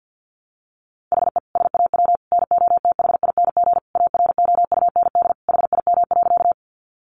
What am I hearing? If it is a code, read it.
Morse code, heard as a keyed tone, about 50 wpm, 700 Hz. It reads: HE SAW NOTHING AROUND HIM.
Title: Sentences from Top 600 Words 50wpm